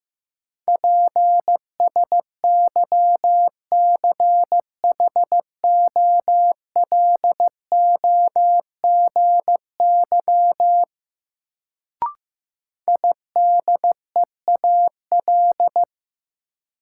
Morse code, heard as a keyed tone, about 15 wpm, 700 Hz. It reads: PSYCHOLOGY IDEAL
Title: Sentences from Top 1400 Words - New Word Drill Alt - 15wpm